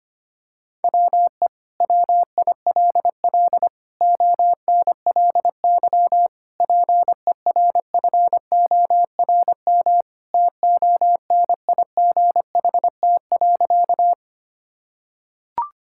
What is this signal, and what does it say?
Morse code, heard as a keyed tone, about 25 wpm, 700 Hz. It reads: WE WILL ONLY PERFORM TONIGHT.
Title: Sentences from Top 1000 Words - Review Part 5 - 25wpm